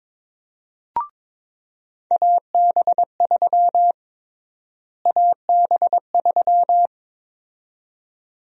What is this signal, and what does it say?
Morse code, heard as a keyed tone, about 23 wpm, 700 Hz. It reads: AB3 AB3
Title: USA Call Sign Prefixes - Repeated 2x - 22wpm